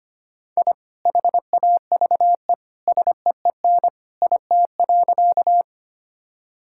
Morse code, heard as a keyed tone, about 25 wpm, 700 Hz. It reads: I HAVE SEEN IT.
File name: practice / 25wpm / Sentences with 2-Word Phrases - 25wpm